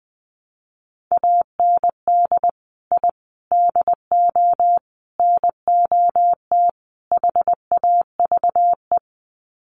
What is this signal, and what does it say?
Morse code, heard as a keyed tone, about 20 wpm, 700 Hz. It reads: AND I DO NOT HAVE
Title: Sets of 5 Words Encore 20wpm